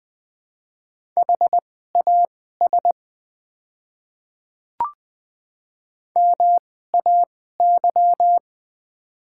Morse code, heard as a keyed tone, about 20 wpm, spaced as for 14 wpm, 700 Hz. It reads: HAS MAY
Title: 2x Character Spacing - Top 100 Words - 20wpm